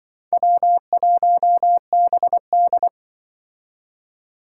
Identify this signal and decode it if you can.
Morse code, heard as a keyed tone, about 24 wpm, 700 Hz. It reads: W1BD
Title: Speed Racing - USA 1x2 Call Signs 20wpm